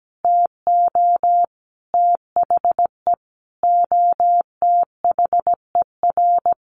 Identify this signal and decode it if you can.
Morse code, heard as a keyed tone, about 17 wpm, 700 Hz. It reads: TO THE OTHER
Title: Sentences from Top 1400 Words 17wpm